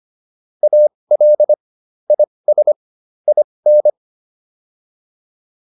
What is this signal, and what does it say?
Morse code, heard as a keyed tone, about 25 wpm, spaced as for 20 wpm, 600 Hz. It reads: AL IS IN